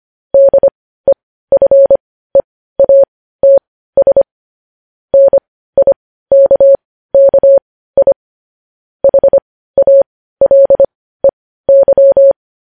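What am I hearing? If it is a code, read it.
Morse code, heard as a keyed tone, about 25 wpm, spaced as for 15 wpm, 550 Hz. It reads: DEFEATS NIKKI HALEY
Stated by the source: Morse news headlines practice at 15.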